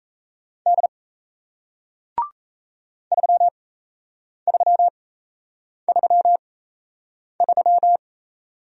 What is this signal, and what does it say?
Morse code, heard as a keyed tone, about 38 wpm, spaced as for 8 wpm, 700 Hz. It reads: D 3333